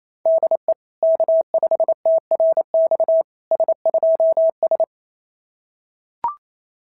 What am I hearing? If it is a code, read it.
Morse code, heard as a keyed tone, about 27 wpm, 650 Hz. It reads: DE K5TRX S2S
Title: SOTA QSO Practice 28wpm